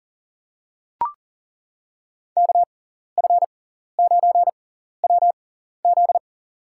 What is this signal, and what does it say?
Morse code, heard as a keyed tone, about 40 wpm, spaced as for 14 wpm, 700 Hz. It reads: KF9WZ
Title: USA Call Signs - 5 Characters - 6x Character Spacing - 40wpm